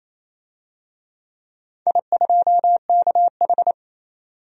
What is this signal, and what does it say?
Morse code, heard as a keyed tone, about 28 wpm, 690 Hz. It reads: I2KH
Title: DXCC-100 Call Signs - 4 Characters - 28wpm